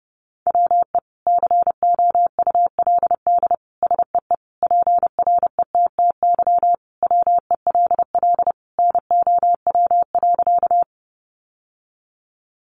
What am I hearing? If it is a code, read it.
Morse code, heard as a keyed tone, about 30 wpm, 700 Hz. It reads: WE COULD SEE PRETTY WELL NOW.
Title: Sentences from Top 1000 Words 30wpm